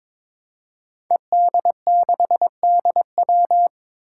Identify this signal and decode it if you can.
Morse code, heard as a keyed tone, about 22 wpm, 700 Hz. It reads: ED6DW
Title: Five Challenging Letter-Number Combinations 22wpm